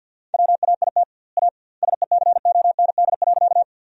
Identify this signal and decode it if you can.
Morse code, heard as a keyed tone, about 50 wpm, 700 Hz. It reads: WAIT A SECOND.
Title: 2x Word-Spacing - Sentences from Top 500 Words - 50wpm